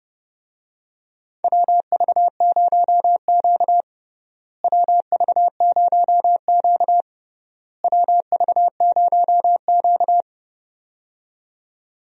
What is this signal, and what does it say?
Morse code, heard as a keyed tone, about 30 wpm, 700 Hz. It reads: WV0Q WV0Q WV0Q
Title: USA Call Signs - 4 Characters - Repeated 3x - 30wpm